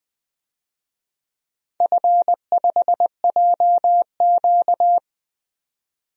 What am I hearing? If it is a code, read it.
Morse code, heard as a keyed tone, about 20 wpm, 700 Hz. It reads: F5JQ
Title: DXCC-100 Call Signs - 4 Characters - 20wpm